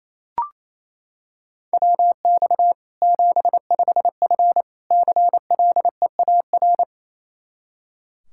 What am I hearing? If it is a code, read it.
Morse code, heard as a keyed tone, about 28 wpm, 700 Hz. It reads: WX 75F CLEAR
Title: Practice QSO - Exchange 2 - Info - 28wpm